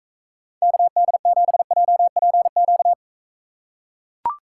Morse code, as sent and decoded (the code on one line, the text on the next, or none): -.- -.. --... .--- .--. --.-
KD7JPQ